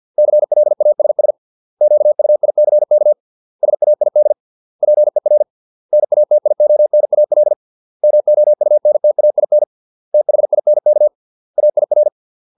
Morse code, heard as a keyed tone, about 50 wpm, 600 Hz. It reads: CLASS QUICK SAID PER NATIONAL MOUNTAIN THINK AIR